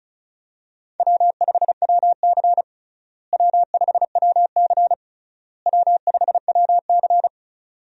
Morse code, heard as a keyed tone, about 36 wpm, 700 Hz. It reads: W5WC W5WC W5WC
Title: USA Call Signs - 4 Characters - Repeated 3x - 35wpm